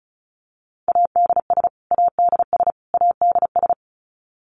Morse code, heard as a keyed tone, about 34 wpm, 700 Hz. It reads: ADS ADS ADS
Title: Abbreviations Repeated 3x - 35wpm